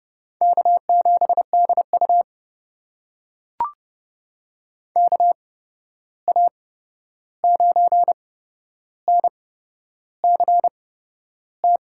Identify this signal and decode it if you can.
Morse code, heard as a keyed tone, about 30 wpm, spaced as for 11 wpm, 700 Hz. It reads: K7DU KA9NCT
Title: USA Call Signs - 8x Character Spacing - 30wpm